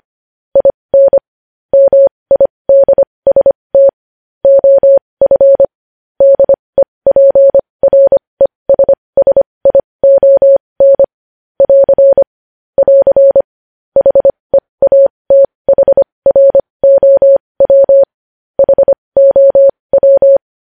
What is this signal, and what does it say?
Morse code, heard as a keyed tone, about 25 wpm, spaced as for 20 wpm, 550 Hz. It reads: IN MIDST OF DEPRESSION <AR> <AR> HEATHROW HOW
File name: podcast / News headlines-1x-25x20